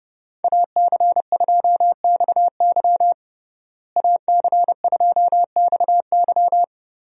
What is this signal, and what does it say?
Morse code, heard as a keyed tone, about 30 wpm, 700 Hz. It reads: AC2XY AC2XY
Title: USA Call Signs - 5 Characters - Repeated 2x - 30wpm